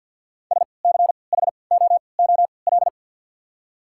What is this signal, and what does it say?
Morse code, heard as a keyed tone, about 54 wpm, spaced as for 29 wpm, 700 Hz. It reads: SCHOOL
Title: 3x Character Spacing - Top 300 Words - 50wpm